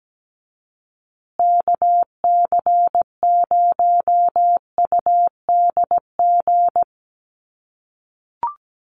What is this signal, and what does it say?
Morse code, heard as a keyed tone, about 17 wpm, 700 Hz. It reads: KC0UDG
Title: USA Call Signs Repeated 3x 17wpm